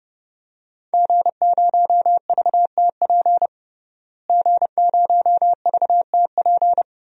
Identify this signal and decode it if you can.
Morse code, heard as a keyed tone, about 30 wpm, 700 Hz. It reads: G0VTP G0VTP
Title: DXCC-100 Call Signs - 5 Characters - Repeated 3x - 30wpm